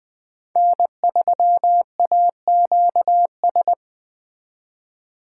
Morse code, heard as a keyed tone, about 20 wpm, 700 Hz. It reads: N3AQS
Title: Speed Racing - USA 1x3 Call Signs 17wpm